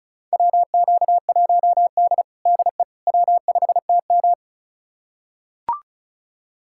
Morse code, heard as a keyed tone, about 34 wpm, 700 Hz. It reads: WQ1D DE W5TM
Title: USA Callsign DE Callsign - 4 Characters - 35wpm